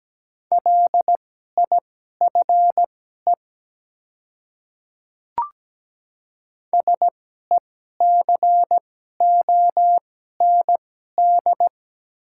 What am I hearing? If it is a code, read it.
Morse code, heard as a keyed tone, about 17 wpm, spaced as for 12 wpm, 690 Hz. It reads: LIFE SECOND
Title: 2x Character Spacing - Top 300 Words - 17wpm